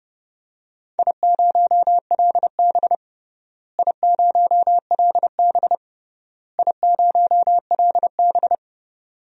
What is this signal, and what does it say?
Morse code, heard as a keyed tone, about 30 wpm, 700 Hz. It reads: I0LB I0LB I0LB
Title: DXCC-100 Call Signs - 4 Characters - Repeated 3x - 30wpm